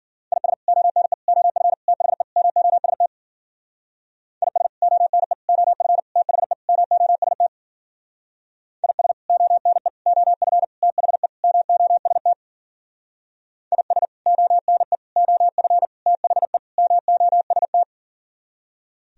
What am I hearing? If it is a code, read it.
Morse code, heard as a keyed tone, about 51 wpm, 700 Hz. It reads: IS ONE OF THE MOST IS ONE OF THE MOST IS ONE OF THE MOST IS ONE OF THE MOST